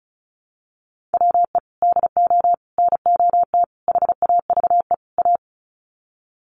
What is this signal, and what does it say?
Morse code, heard as a keyed tone, about 34 wpm, 700 Hz. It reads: WE DO NOT HAVE A
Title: Sets of 5 Words 35wpm